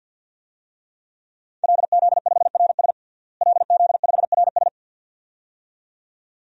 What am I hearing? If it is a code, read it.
Morse code, heard as a keyed tone, about 53 wpm, 700 Hz. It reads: LZ5RS LZ5RS
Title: DXCC-100 Call Signs - 5 Characters - Repeated 2x - 50wpm